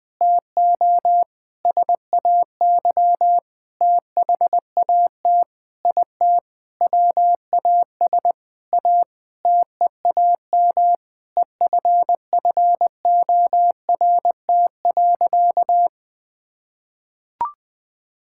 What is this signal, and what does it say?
Morse code, heard as a keyed tone, about 20 wpm, 700 Hz. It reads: TO SAY THAT IT WAS A TEAM EFFORT.
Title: Sentences with 8-Word Phrases - Spoken Beginning - 20wpm